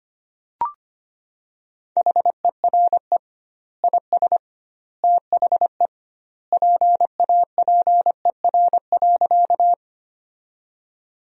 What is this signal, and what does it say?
Morse code, heard as a keyed tone, about 25 wpm, 700 Hz. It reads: HERE IS THE PAPER.
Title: 2x Word-Spacing - Sentences from Top 300 Words - 25wpm